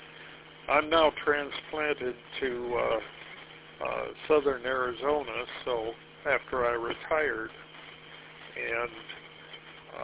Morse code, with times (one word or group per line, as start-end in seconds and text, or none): none